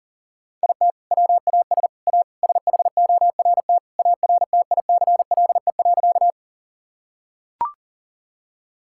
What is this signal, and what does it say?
Morse code, heard as a keyed tone, about 40 wpm, 700 Hz. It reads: IT WAS A SHORT ARTICLE.
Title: Sentences from Top 1000 Words - Review Part 5 - 40wpm